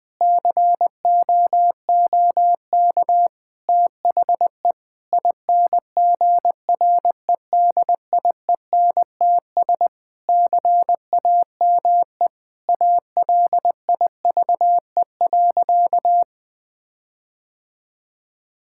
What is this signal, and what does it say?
Morse code, heard as a keyed tone, about 19 wpm, 700 Hz. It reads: COOK THE INGREDIENTS CAME ALIVE.
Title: Sentences with 5-Word Phrases - 20wpm